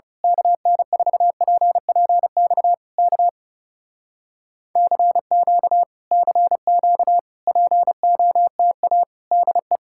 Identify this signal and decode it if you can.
Morse code, heard as a keyed tone, about 28 wpm, 700 Hz. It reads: KN4PPX K CQ CQ POTA DE
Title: Speed-Racing Common QSO Phrases Encore - 30wpm